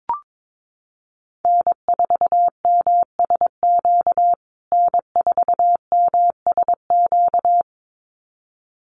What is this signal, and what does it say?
Morse code, heard as a keyed tone, about 22 wpm, 700 Hz. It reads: N4MSQ N4MSQ
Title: USA Call Signs Repeated 2x 22wpm